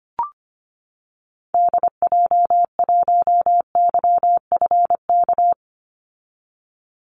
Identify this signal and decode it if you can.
Morse code, heard as a keyed tone, about 25 wpm, 700 Hz. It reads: DJ1YFK
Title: Call Signs - 25wpm